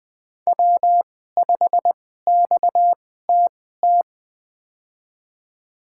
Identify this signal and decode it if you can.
Morse code, heard as a keyed tone, about 20 wpm, spaced as for 15 wpm, 700 Hz. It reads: W5XTT
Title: USA Call Signs - 5 Characters - 2x Character Spacing - 20wpm